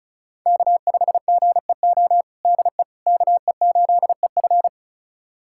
Morse code, heard as a keyed tone, about 34 wpm, 700 Hz. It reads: K5GEO DE KE8EF
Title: USA Callsign DE Callsign - 5 Characters - 35wpm